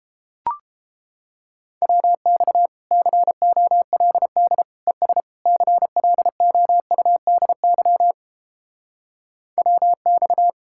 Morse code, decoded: WX COLD ES CLOUDY WX

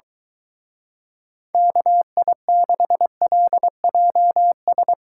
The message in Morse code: KI6LJS

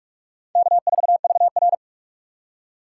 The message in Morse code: K4VR

